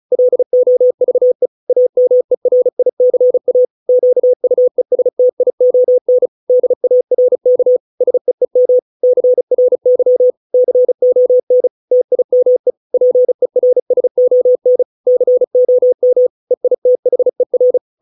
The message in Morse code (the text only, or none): LOVE AMERICA QUESTION DARK SEEM CRY CON TIME PERSON COM EITHER